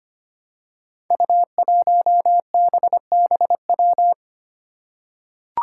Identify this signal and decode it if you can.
Morse code, heard as a keyed tone, about 25 wpm, 700 Hz. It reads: U1BBW
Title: Five Challenging Letter-Number Combinations 25wpm